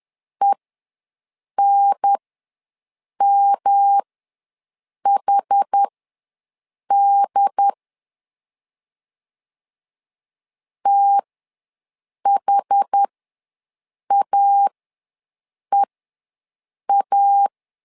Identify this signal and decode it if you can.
Morse code, heard as a keyed tone, about 10 wpm, spaced as for 6 wpm, 800 Hz. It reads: ENMHD THAEA
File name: LEZIONI / 3 MWRO / 3 M ws 10wpm 2121